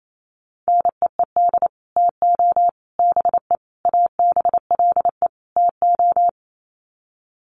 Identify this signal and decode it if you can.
Morse code, heard as a keyed tone, about 28 wpm, 700 Hz. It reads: NEED TO BE ABLE TO
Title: Sets of 5 Words Encore 28wpm